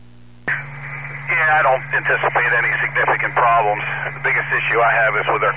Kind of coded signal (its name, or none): Emergency Alert System (SAME) tones